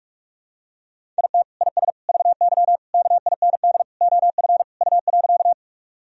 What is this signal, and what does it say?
Morse code, heard as a keyed tone, about 47 wpm, 700 Hz. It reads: IT IS VY KIND OF U.
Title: Sentences from Top 200 Words and Abbreviations 45wpm